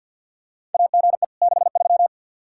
Morse code, heard as a keyed tone, about 52 wpm, 700 Hz. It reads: AGE 63
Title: Practice QSO - Exchange 3 - More Info - 50wpm